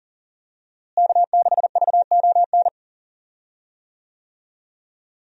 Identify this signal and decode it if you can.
Morse code, heard as a keyed tone, about 40 wpm, 700 Hz. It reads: K6VON
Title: USA Call Signs - 5 Characters - Repeated 3x - 40wpm